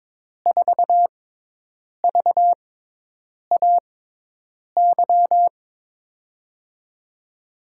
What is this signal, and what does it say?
Morse code, heard as a keyed tone, about 22 wpm, spaced as for 8 wpm, 700 Hz. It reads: 4VAY
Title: USA Call Signs - 6 Characters - 6x Character Spacing - 22wpm